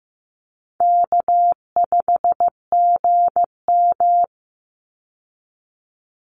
Morse code, heard as a keyed tone, about 15 wpm, 700 Hz. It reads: K5GM
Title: Call Signs Repeated 3x - 15wpm